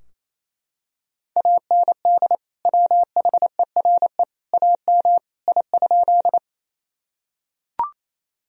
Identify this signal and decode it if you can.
Morse code, heard as a keyed tone, about 28 wpm, 700 Hz. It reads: AND WHERE AM I?
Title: Sentences from Top 1000 Words - Review Part 2 - 28wpm